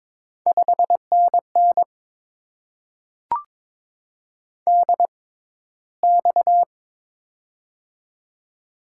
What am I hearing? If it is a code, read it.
Morse code, heard as a keyed tone, about 22 wpm, spaced as for 13 wpm, 700 Hz. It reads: 5NN DX
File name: Top 90 QSO Elements - 6x Character Spacing - 22wpm